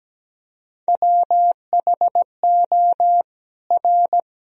Morse code, heard as a keyed tone, about 17 wpm, 700 Hz. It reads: WHO R
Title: Sentences from Top 100 Words and Abbreviations 17wpm